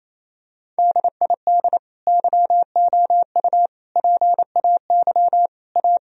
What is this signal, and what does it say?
Morse code, heard as a keyed tone, about 28 wpm, 700 Hz. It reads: DID YOU PAY A
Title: Sentences from Top 1000 Words - Review Part 4 - 28wpm